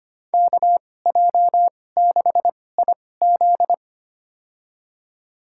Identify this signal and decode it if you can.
Morse code, heard as a keyed tone, about 25 wpm, spaced as for 18 wpm, 700 Hz. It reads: KJ6IZ
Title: USA Call Signs - 2x Character Spacing - 25wpm